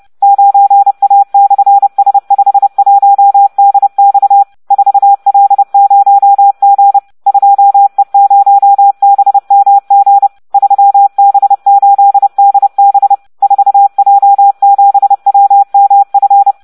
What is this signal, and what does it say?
Morse code, heard as a keyed tone, about 30 wpm, 800 Hz, with radio noise. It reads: 9A/S51DX 4L0G 2E0BMG 3B8DB 4J7WMF